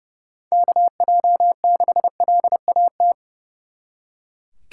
Morse code, heard as a keyed tone, about 36 wpm, 700 Hz. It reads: KJ6LAT
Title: Speed Racing - USA 2x3 Call Signs 30wpm